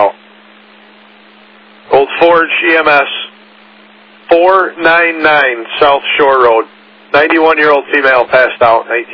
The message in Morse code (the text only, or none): none